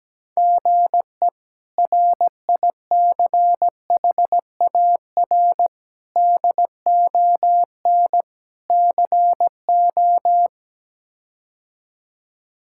Morse code, heard as a keyed tone, about 17 wpm, 700 Hz. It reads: GE RICHAR DON CO
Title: K1USN SST QSO Practice - 17wpm